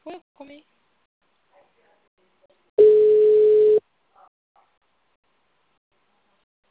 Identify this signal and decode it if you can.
Ringback tone (Europe, 425 Hz), with speech.
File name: out- -2005- - - .16065